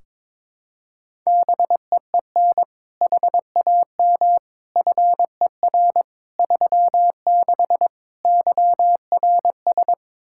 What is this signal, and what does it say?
Morse code, heard as a keyed tone, about 22 wpm, 700 Hz. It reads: BEEN HAM FER 36 YRS
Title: Practice QSO - Exchange 3 - More Info - 22wpm